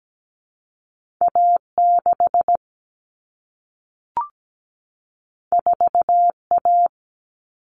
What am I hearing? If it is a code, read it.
Morse code, heard as a keyed tone, about 17 wpm, 700 Hz. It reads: A6 4A
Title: All Cut Numbers - 2-Digit 17wpm